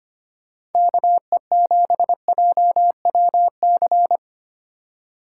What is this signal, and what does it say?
Morse code, heard as a keyed tone, about 33 wpm, 700 Hz. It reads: KE7JWC